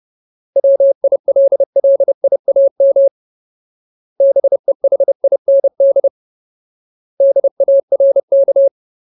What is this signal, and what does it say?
Morse code, heard as a keyed tone, about 30 wpm, 550 Hz. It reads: WILLIAM BEHIND DARK
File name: Ultimate Word List / 30WPM Ultimate